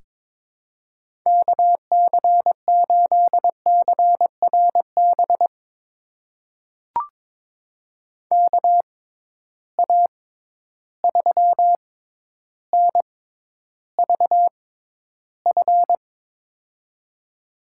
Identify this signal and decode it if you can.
Morse code, heard as a keyed tone, about 22 wpm, spaced as for 11 wpm, 700 Hz. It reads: KC8CRB KA3NVF
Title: USA Call Signs - 6 Characters - 6x Character Spacing - 22wpm